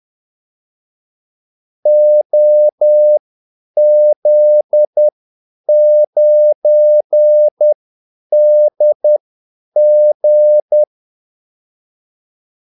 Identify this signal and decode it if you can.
Morse code, heard as a keyed tone, about 10 wpm, spaced as for 8 wpm, 600 Hz. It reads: OZ9DG
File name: Callsigns-10-8wpm